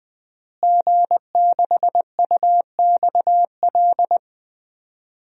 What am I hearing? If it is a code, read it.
Morse code, heard as a keyed tone, about 20 wpm, 700 Hz. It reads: G6UXL
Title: DXCC-100 Call Signs - 5 Characters - Repeated 3x - 20wpm